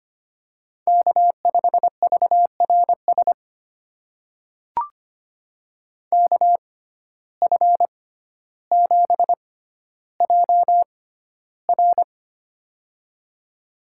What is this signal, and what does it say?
Morse code, heard as a keyed tone, about 25 wpm, spaced as for 13 wpm, 700 Hz. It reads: K5VRS KF7JR
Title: USA Call Signs - 5 Characters - 6x Character Spacing - 25wpm